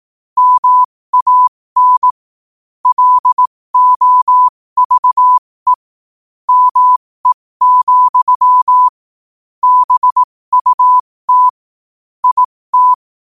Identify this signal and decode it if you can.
Morse code, heard as a keyed tone, about 18 wpm, spaced as for 15 wpm, 1000 Hz. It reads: MAN LOVE ME, BUT IT
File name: qotdfiles / 15wpm